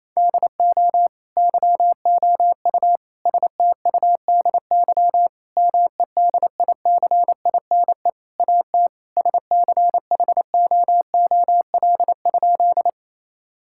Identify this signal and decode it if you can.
Morse code, heard as a keyed tone, about 28 wpm, 700 Hz. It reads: DO YOU STUDY MEDICINE AT SCHOOL?
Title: Sentences from Top 1000 Words - Review Part 6 - 28wpm